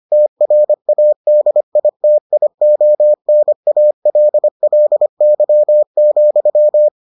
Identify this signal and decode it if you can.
Morse code, heard as a keyed tone, about 25 wpm, 600 Hz. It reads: TRADITIONALLY,